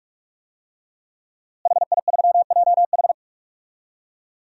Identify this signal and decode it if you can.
Morse code, heard as a keyed tone, about 46 wpm, 700 Hz. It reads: HI3JH
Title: DXCC-100 Call Signs - 5 Characters - Repeated 2x - 45wpm